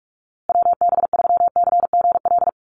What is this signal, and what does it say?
Morse code, heard as a keyed tone, about 44 wpm, 700 Hz. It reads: WB3CGL
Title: USA Call Signs Repeated 2x 45wpm